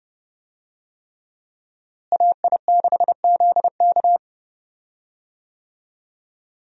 Morse code, heard as a keyed tone, about 30 wpm, 700 Hz. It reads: AI6ZK
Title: USA Call Signs - 5 Characters - Repeated 3x - 30wpm